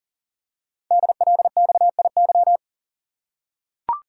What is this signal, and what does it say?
Morse code, heard as a keyed tone, about 39 wpm, 700 Hz. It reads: DLXIY
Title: Code Groups - 5 Letters - 40wpm